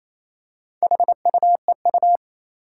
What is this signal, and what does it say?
Morse code, heard as a keyed tone, about 27 wpm, 700 Hz. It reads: HUEU
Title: Four Challenging Letter-Number Combinations 28wpm